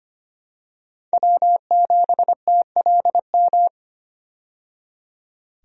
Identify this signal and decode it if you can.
Morse code, heard as a keyed tone, about 25 wpm, 700 Hz. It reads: W7TLM
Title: USA Call Signs - 5 Characters - 25wpm